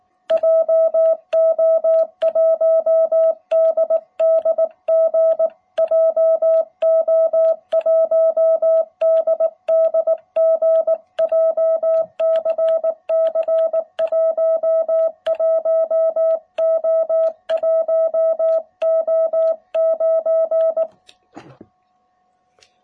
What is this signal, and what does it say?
Morse code, heard as a keyed tone, about 19 wpm, 660 Hz. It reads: JO1DDG JO1DDG JCC 11O1O9